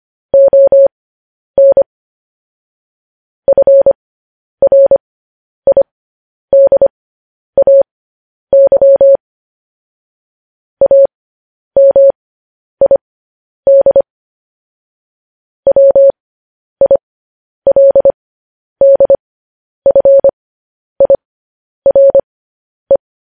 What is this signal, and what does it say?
Morse code, heard as a keyed tone, about 25 wpm, spaced as for 10 wpm, 550 Hz. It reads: ON FRIDAY AMID WILDFIRE
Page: news headlines in morse code at 10 wpm